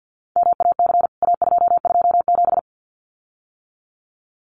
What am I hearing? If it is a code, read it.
Morse code, heard as a keyed tone, about 48 wpm, 700 Hz. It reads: MAC A227
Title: CWT Exchange 50wpm